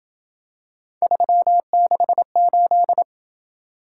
Morse code, heard as a keyed tone, about 26 wpm, 700 Hz. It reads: 368